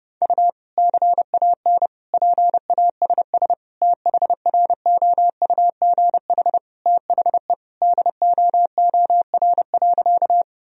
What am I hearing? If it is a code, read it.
Morse code, heard as a keyed tone, about 30 wpm, 700 Hz. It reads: U CAN PASS THROUGH THE DOOR.
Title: Sentences from Top 400 Words and Abbreviations 30wpm